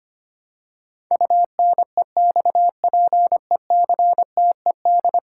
Morse code, heard as a keyed tone, about 25 wpm, 700 Hz. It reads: UNEXPECTED
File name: 10 Letter Words - First 3 Letters Given - 25wpm